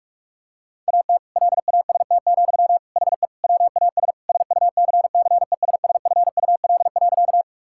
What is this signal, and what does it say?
Morse code, heard as a keyed tone, about 42 wpm, 700 Hz. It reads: AT LAST, HE WAS SUCCESSFUL.
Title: Sentences from Top 1000 Words - Review Part 6 - 45wpm